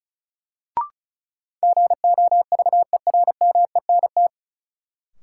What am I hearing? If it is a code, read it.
Morse code, heard as a keyed tone, about 36 wpm, 700 Hz. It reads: GOVERMENT
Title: Sentences from Top 1300 Words - New Word Drill - 35wpm